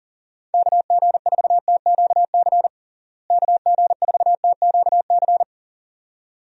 Morse code, heard as a keyed tone, about 40 wpm, 700 Hz. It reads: KG4TQC KG4TQC
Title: USA Call Signs - 6 Characters - Repeated 2x - 40wpm